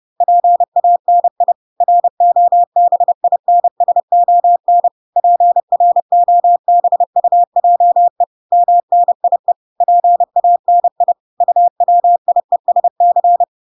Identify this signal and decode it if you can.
Morse code, heard as a keyed tone, about 30 wpm, 700 Hz. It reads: PANI ROBINSON PROBUJE MNIE PANI UWIESC